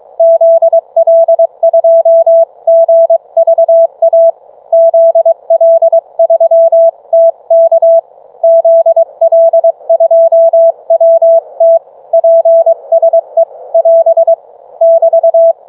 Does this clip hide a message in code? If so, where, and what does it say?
Yes: Morse code from 0.2 to 15.5 s, heard as a keyed tone, about 20 wpm, 660 Hz. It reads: ZL2GVA ZL3TK ZL2WT PSE <AS> <BT>